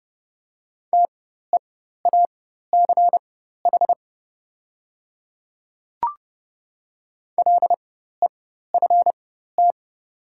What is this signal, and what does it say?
Morse code, heard as a keyed tone, about 30 wpm, spaced as for 14 wpm, 700 Hz. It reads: TEACH LEFT
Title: 4x Character Spacing - Top 500 Words - 30wpm